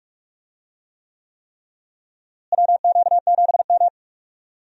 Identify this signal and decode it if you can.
Morse code, heard as a keyed tone, about 44 wpm, 700 Hz. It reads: WQ7M
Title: USA Call Signs - 4 Characters - Repeated 3x - 45wpm